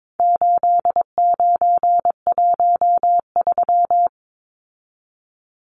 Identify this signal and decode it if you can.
Morse code, heard as a keyed tone, about 22 wpm, 700 Hz. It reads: 8913
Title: Numbers - 4-Digit 22wpm